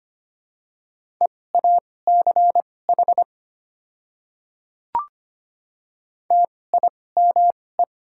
Morse code, heard as a keyed tone, about 25 wpm, spaced as for 18 wpm, 690 Hz. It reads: EACH TIME
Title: 2x Character Spacing - Top 100 Words - 25wpm